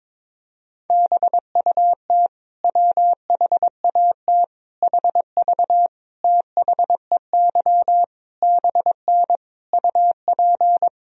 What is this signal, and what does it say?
Morse code, heard as a keyed tone, about 22 wpm, 700 Hz. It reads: BUT WHAT HV THEY BN UP
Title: Sentences from Top 100 Words and Abbreviations 22wpm